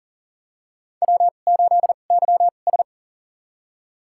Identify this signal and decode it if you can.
Morse code, heard as a keyed tone, about 40 wpm, spaced as for 29 wpm, 700 Hz. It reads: W8YS
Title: USA Call Signs - 4 Characters - 2x Character Spacing - 40wpm